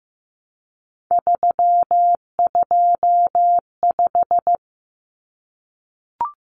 Morse code, heard as a keyed tone, about 15 wpm, 700 Hz. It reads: 325
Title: Numbers - 3-Digit 15wpm